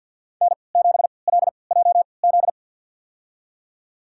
Morse code, heard as a keyed tone, about 46 wpm, spaced as for 28 wpm, 700 Hz. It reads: N7LJZ
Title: USA Call Signs - 5 Characters - 3x Character Spacing - 50wpm